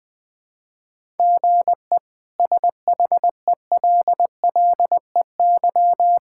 Morse code, heard as a keyed tone, about 20 wpm, 700 Hz. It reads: GE SHELLEY
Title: K1USN SST QSO Practice - 20wpm